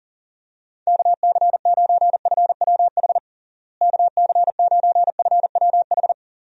Morse code, heard as a keyed tone, about 40 wpm, 700 Hz. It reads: KC9FWH KC9FWH
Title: USA Call Signs - 6 Characters - Repeated 2x - 40wpm